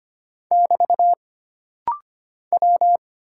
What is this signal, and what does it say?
Morse code, heard as a keyed tone, about 25 wpm, 700 Hz. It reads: <BT> W
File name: Single Character (Letter:Number:Punctuation:Prosign) - Rapid-Fire - 25wpm